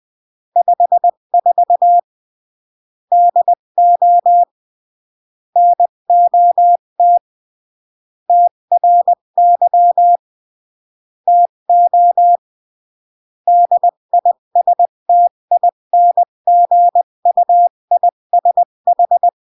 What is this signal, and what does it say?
Morse code, heard as a keyed tone, about 20 wpm, 700 Hz. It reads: 54 DO NOT TRY TO DISTINGUISH
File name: CW / jodorowsky / 82 commandments 20-15